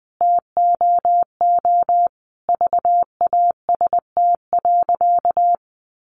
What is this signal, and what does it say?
Morse code, heard as a keyed tone, about 20 wpm, 700 Hz. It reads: TOO VAST.
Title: Sentences from Top 1300 Words 20wpm